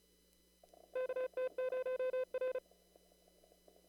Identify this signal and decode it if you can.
Morse code, heard as a keyed tone, about 34 wpm, 470 Hz. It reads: KT0R
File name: CW / kt5j wpx cw-op-k5pi